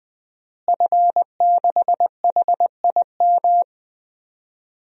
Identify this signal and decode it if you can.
Morse code, heard as a keyed tone, about 20 wpm, 700 Hz. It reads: F6HIM
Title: DXCC-100 Call Signs - 5 Characters - Repeated 2x - 20wpm